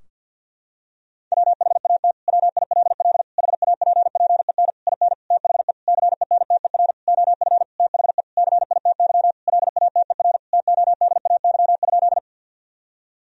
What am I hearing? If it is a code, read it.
Morse code, heard as a keyed tone, about 52 wpm, 700 Hz. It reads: WHAT WILL HAPPEN IN THE CENTER OF THE CITY LATER TODAY?
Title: Sentences from Top 1000 Words - Review Part 2 - 50wpm